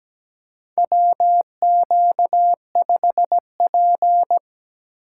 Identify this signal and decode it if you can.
Morse code, heard as a keyed tone, about 17 wpm, 700 Hz. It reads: WQ5P
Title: USA Call Signs - 4 Characters - 8x Character Spacing - 17wpm